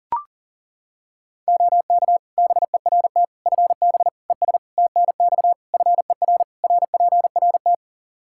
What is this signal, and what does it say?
Morse code, heard as a keyed tone, about 40 wpm, 700 Hz. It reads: OK BERT FB ES TNX FER RPRT
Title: Practice QSO - Exchange 2 - Info - 40wpm